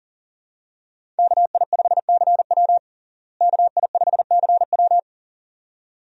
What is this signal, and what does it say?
Morse code, heard as a keyed tone, about 40 wpm, 700 Hz. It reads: KI5CW KI5CW
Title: USA Call Signs - 5 Characters - Repeated 2x - 40wpm